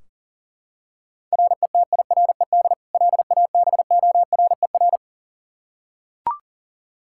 Morse code, heard as a keyed tone, about 40 wpm, 700 Hz. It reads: RETIRED LABORER
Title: Practice QSO - Exchange 3 - More Info - 40wpm